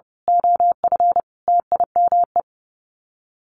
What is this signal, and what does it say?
Morse code, heard as a keyed tone, about 30 wpm, 700 Hz. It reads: OF TIME
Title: Sets of 5 Words Encore 30wpm